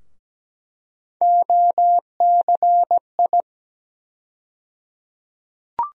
Morse code, heard as a keyed tone, about 17 wpm, 700 Hz. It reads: OCI
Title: USA Call Sign Postfixes - 3 Letters 17wpm